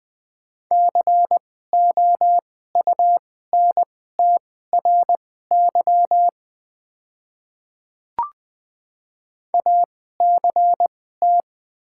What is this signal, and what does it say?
Morse code, heard as a keyed tone, about 20 wpm, spaced as for 14 wpm, 690 Hz. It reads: COUNTRY ACT
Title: 2x Character Spacing - Top 300 Words - 20wpm